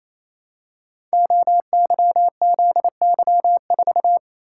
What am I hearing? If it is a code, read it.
Morse code, heard as a keyed tone, about 27 wpm, 700 Hz. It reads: OYZY4
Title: Code Groups - 5 Letter:Numbers - 28wpm